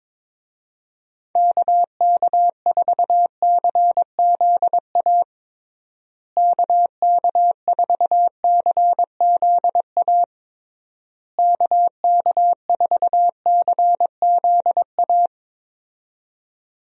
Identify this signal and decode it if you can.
Morse code, heard as a keyed tone, about 21 wpm, 700 Hz. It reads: KK4CZA KK4CZA KK4CZA
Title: USA Call Signs - 6 Characters - Repeated 3x - 22wpm